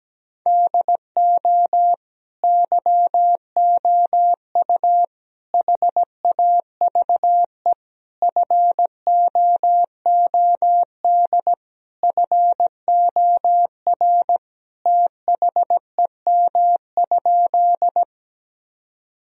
Morse code, heard as a keyed tone, about 17 wpm, 700 Hz. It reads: DO YOU HAVE FOOD FOR THEM?
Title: Sentences from Top 1000 Words - Review Part 1 - 17wpm